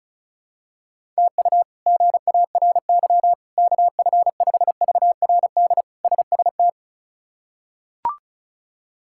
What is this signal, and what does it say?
Morse code, heard as a keyed tone, about 36 wpm, 700 Hz. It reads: TU GARY KF5VRD SST
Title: K1USN SST QSO Practice - 35wpm